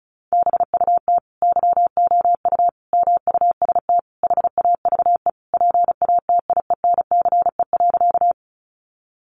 Morse code, heard as a keyed tone, about 34 wpm, 700 Hz. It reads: BUT YOU MUST HAVE PATIENCE.
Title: Sentences from Top 1300 Words 35wpm